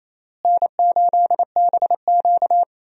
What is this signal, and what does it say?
Morse code, heard as a keyed tone, about 28 wpm, 700 Hz. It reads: N8BQ
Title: USA Call Signs - 4 Characters - Repeated 3x - 28wpm